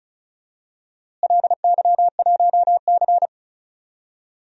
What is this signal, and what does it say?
Morse code, heard as a keyed tone, about 36 wpm, 700 Hz. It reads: LY1C
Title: DXCC-100 Call Signs - 4 Characters - Repeated 3x - 35wpm